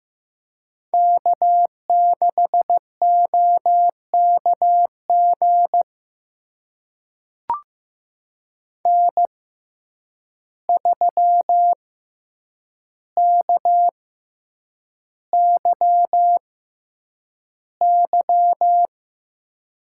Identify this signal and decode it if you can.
Morse code, heard as a keyed tone, about 15 wpm, spaced as for 8 wpm, 700 Hz. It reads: K6OKG N3KYY
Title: USA Call Signs - 6x Character Spacing - 15wpm